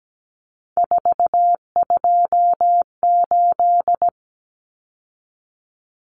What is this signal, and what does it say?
Morse code, heard as a keyed tone, about 17 wpm, 700 Hz. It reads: 428
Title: Common Cut Numbers - 3-Digit 17wpm